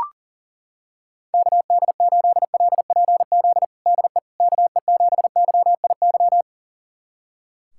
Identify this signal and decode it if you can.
Morse code, heard as a keyed tone, about 40 wpm, 700 Hz. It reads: KD8LPZ DE KE7YIY
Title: USA Callsign DE Callsign - 6 Characters - 40wpm